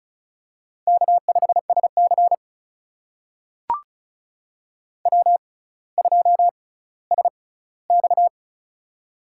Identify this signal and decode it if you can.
Morse code, heard as a keyed tone, about 36 wpm, spaced as for 18 wpm, 700 Hz. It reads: K5SC W2SX
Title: USA Call Signs - 4 Characters - 6x Character Spacing - 35wpm